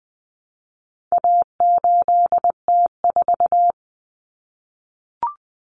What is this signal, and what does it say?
Morse code, heard as a keyed tone, about 20 wpm, 700 Hz. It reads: A8T4
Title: Common Cut Numbers - 4-Digit 20wpm